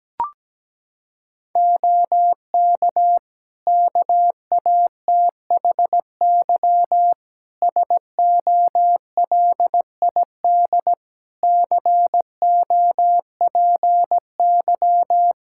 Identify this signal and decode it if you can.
Morse code, heard as a keyed tone, about 17 wpm, 700 Hz. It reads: OK KATHY SOLID COPY
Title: Practice QSO - Exchange 3 - More Info - 17wpm